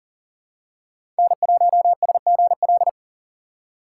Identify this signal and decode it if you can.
Morse code, heard as a keyed tone, about 40 wpm, 700 Hz. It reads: N1SGL
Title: USA Call Signs - 5 Characters - 4x Character Spacing - 40wpm